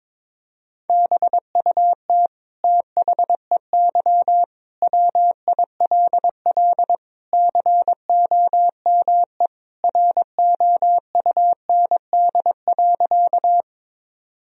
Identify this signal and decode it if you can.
Morse code, heard as a keyed tone, about 22 wpm, 700 Hz. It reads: BUT THEY WILL COME ROUND.
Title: Sentences from Top 1000 Words - Review Part 1 - 22wpm